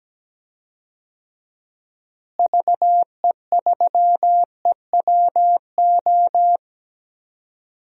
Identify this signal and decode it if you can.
Morse code, heard as a keyed tone, about 17 wpm, 700 Hz. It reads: VE3EWO
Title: DXCC-100 Call Signs - 6 Characters - Repeated 3x - 17wpm